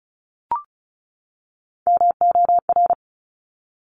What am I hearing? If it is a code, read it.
Morse code, heard as a keyed tone, about 36 wpm, 700 Hz. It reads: MOR
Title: Top Three Letter Combinations in Words 35wpm